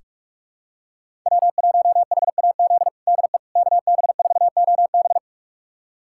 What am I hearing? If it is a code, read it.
Morse code, heard as a keyed tone, about 46 wpm, 700 Hz. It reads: W1HAZ DE KB4OB
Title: USA Callsign DE Callsign - 5 Characters - 45wpm